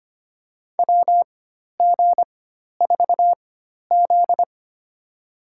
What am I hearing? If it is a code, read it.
Morse code, heard as a keyed tone, about 25 wpm, spaced as for 12 wpm, 700 Hz. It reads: WG4Z
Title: USA Call Signs - 4x Character Spacing - 25wpm